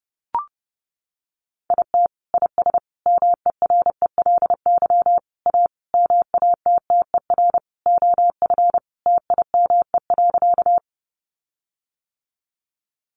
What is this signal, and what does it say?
Morse code, heard as a keyed tone, about 28 wpm, 700 Hz. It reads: IT IS MERELY A MATTER OF TIME.
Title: Sentences from Top 1400 Words 30wpm